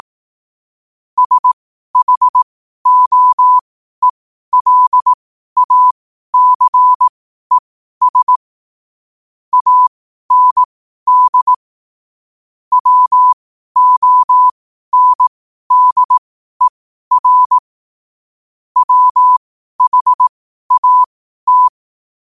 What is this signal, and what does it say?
Morse code, heard as a keyed tone, about 18 wpm, spaced as for 12 wpm, 1000 Hz. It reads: SHOELACES AND WONDER WHAT
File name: qotdfiles / 12wpm